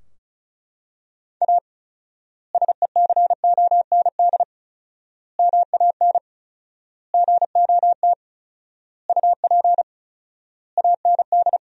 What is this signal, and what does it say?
Morse code, heard as a keyed tone, about 36 wpm, 700 Hz. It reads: A SECOND MAN GOT UP AND
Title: 4x Word-Spacing - Sentences from Top 300 Words - 35wpm